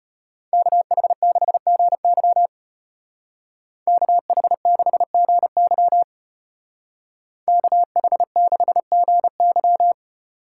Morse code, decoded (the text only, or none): KH6GY KH6GY KH6GY